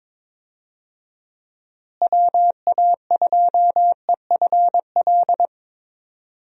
Morse code, heard as a keyed tone, about 22 wpm, 700 Hz. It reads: WA2EFL
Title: USA Call Signs - 6 Characters - Repeated 3x - 22wpm